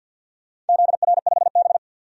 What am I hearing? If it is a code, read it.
Morse code, heard as a keyed tone, about 53 wpm, 700 Hz. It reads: ZR5B
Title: DXCC-100 Call Signs - 4 Characters - 50wpm